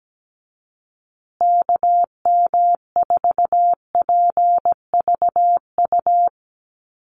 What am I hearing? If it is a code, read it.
Morse code, heard as a keyed tone, about 17 wpm, 700 Hz. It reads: KM4PVU
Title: USA Call Signs Repeated 3x 17wpm